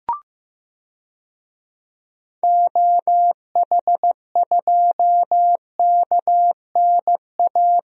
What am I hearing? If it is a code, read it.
Morse code, heard as a keyed tone, about 15 wpm, 700 Hz. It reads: OH2KNA
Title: DXCC-100 Call Signs - 6 Characters - 15wpm